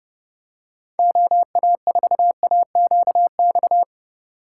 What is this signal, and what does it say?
Morse code, heard as a keyed tone, about 29 wpm, 700 Hz. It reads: OA4AQX